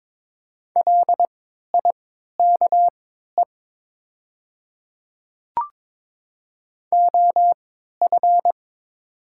Morse code, heard as a keyed tone, about 22 wpm, spaced as for 12 wpm, 700 Hz. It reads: LIKE OF
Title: 3x Character Spacing - Top 200 Words - 22wpm